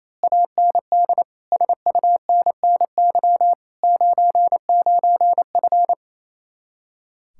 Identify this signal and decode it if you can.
Morse code, heard as a keyed tone, about 26 wpm, 700 Hz. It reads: AND SUNNY 99F
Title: Speed-Racing Common QSO Phrases - 28wpm